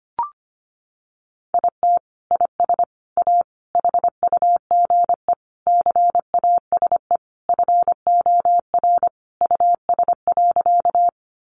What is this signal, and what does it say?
Morse code, heard as a keyed tone, about 25 wpm, 700 Hz. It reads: IT IS A HUGE CASE FOR US.
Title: Sentences from Top 1100 Words 25wpm